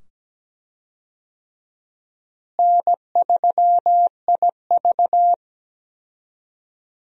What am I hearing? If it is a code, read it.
Morse code, heard as a keyed tone, about 17 wpm, 700 Hz. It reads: N3IV
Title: USA Call Signs - 4 Characters - Repeated 2x - 17wpm